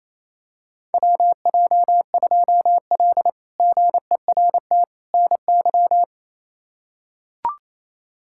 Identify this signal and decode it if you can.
Morse code, heard as a keyed tone, about 27 wpm, 700 Hz. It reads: WJ2L GERT NY
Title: K1USN SST QSO Practice - 28wpm